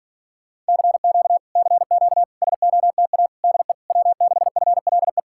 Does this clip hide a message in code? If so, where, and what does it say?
Yes: Morse code from 0.7 to 5.2 s, heard as a keyed tone, about 52 wpm, 700 Hz. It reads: CQ CQ SOTA DE W6FLE